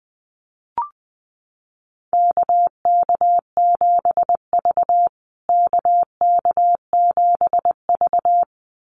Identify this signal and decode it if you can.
Morse code, heard as a keyed tone, about 20 wpm, 700 Hz. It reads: KK7V KK7V
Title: USA Call Signs Repeated 2x 20wpm